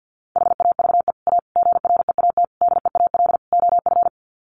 Morse code, heard as a keyed tone, about 50 wpm, 700 Hz. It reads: HAVE A GREAT DEAL OF
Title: Sets of 5 Words Encore2 50wpm